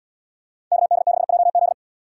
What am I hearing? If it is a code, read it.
Morse code, heard as a keyed tone, about 77 wpm, 700 Hz. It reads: KD6JZ